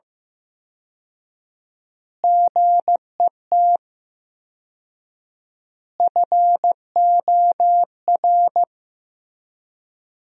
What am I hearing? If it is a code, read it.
Morse code, heard as a keyed tone, about 15 wpm, 700 Hz. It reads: GET FOR
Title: 4x Word-Spacing - Sentences from Top 400 Words - 15wpm